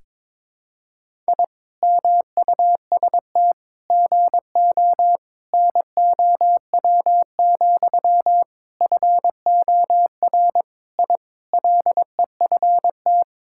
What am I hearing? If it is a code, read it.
Morse code, heard as a keyed tone, about 22 wpm, 700 Hz. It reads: I MUST GO NOW, FOR I LEFT
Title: Sentences from Top 1000 Words - Review Part 5 - 22wpm